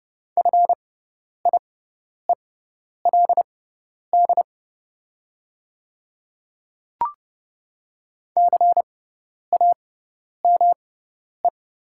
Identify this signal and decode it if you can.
Morse code, heard as a keyed tone, about 30 wpm, spaced as for 10 wpm, 690 Hz. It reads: FIELD CAME
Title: 6x Character Spacing - Top 500 Words - 30wpm